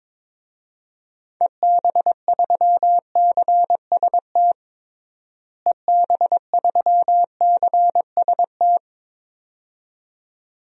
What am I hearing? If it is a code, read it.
Morse code, heard as a keyed tone, about 22 wpm, 700 Hz. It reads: EB3CST EB3CST
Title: DXCC-100 Call Signs - 6 Characters - Repeated 2x - 22wpm